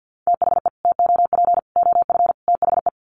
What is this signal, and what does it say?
Morse code, heard as a keyed tone, about 50 wpm, 700 Hz. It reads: THE TOP OF THE
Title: Sets of 4 Words 50wpm